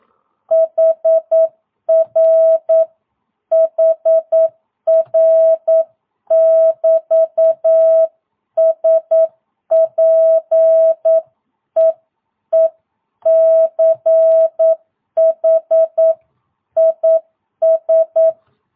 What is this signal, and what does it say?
Morse code, heard as a keyed tone, about 9 wpm, 660 Hz. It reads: HR HR <BT> SPEECH IS
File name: jh8cbh / 2022cw